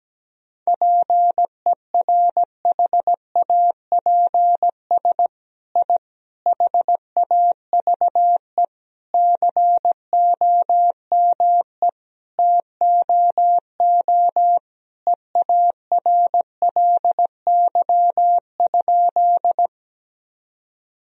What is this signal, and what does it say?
Morse code, heard as a keyed tone, about 17 wpm, 700 Hz. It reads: PERHAPS I HAVE COME TOO EARLY?
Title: Sentences from Top 1000 Words - Review Part 2 - 17wpm